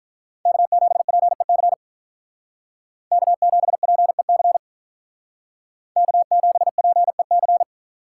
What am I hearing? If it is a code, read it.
Morse code, heard as a keyed tone, about 45 wpm, 700 Hz. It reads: K7PEC K7PEC K7PEC